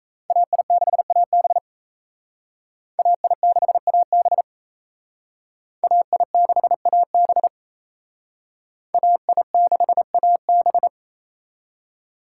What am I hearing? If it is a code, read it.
Morse code, heard as a keyed tone, about 34 wpm, 700 Hz. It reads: AI6AB AI6AB AI6AB AI6AB
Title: Speed Racing - USA 2x2 Call Signs 28wpm